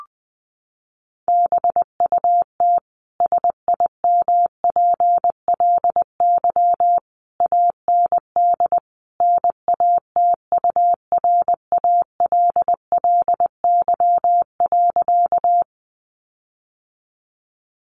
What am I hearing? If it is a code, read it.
Morse code, heard as a keyed tone, about 20 wpm, 700 Hz. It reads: BUT SIMPLY AND NATURALLY.
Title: Sentences from Top 1300 Words 20wpm